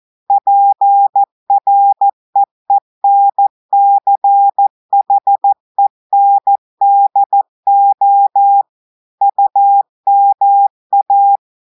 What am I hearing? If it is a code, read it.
Morse code, heard as a keyed tone, about 14 wpm, 800 Hz. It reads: PREENCHENDO UMA
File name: audios / portugais / portugais texte en clair 1 vitesse 14 mots minutes